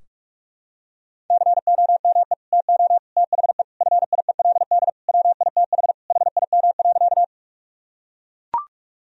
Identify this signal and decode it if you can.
Morse code, heard as a keyed tone, about 41 wpm, 700 Hz. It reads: COME TO THE FIELD WITH HIM.
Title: Sentences from Top 1000 Words - Review Part 2 - 45wpm